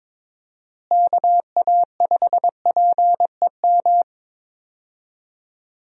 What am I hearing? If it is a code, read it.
Morse code, heard as a keyed tone, about 22 wpm, 700 Hz. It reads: KA5PEM
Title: USA Call Signs - 6 Characters - Repeated 2x - 22wpm